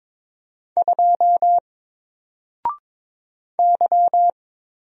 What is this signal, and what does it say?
Morse code, heard as a keyed tone, about 22 wpm, 700 Hz. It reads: 2 Y
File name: Single-Letter-Number-22wpm